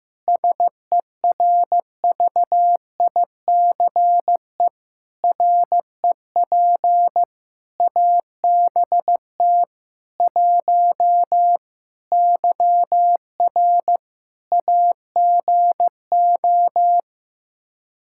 Morse code, heard as a keyed tone, about 15 wpm, 700 Hz. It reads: SERVICE REP ABT 1 YR AGO
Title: Practice QSO - Exchange 3 - More Info - 15wpm